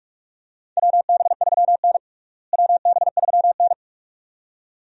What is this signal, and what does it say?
Morse code, heard as a keyed tone, about 46 wpm, 700 Hz. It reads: WB3N WB3N
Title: USA Call Signs - 4 Characters - Repeated 2x - 45wpm